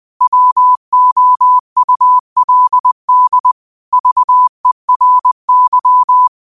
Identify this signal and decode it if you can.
Morse code, heard as a keyed tone, about 20 wpm, 1000 Hz. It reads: WOULD VERY
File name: qotdfiles / 20wpm